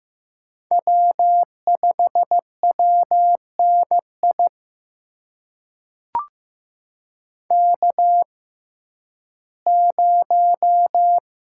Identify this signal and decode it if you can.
Morse code, heard as a keyed tone, about 15 wpm, spaced as for 11 wpm, 700 Hz. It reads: W5WNI K0
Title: USA Call Signs - 5 Characters - 6x Character Spacing - 15wpm